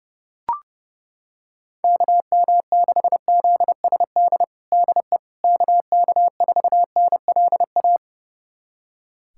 Morse code, heard as a keyed tone, about 30 wpm, 700 Hz. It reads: KM6ZSD DE KK4NLA
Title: USA Callsign DE Callsign - 6 Characters - 30wpm